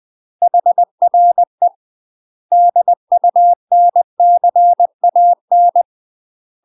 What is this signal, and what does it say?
Morse code, heard as a keyed tone, about 20 wpm, 700 Hz. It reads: HRE DUNCAN
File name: UK MASTER QSOs TRAINING / 20WPM / UK MASTER QSOs 0030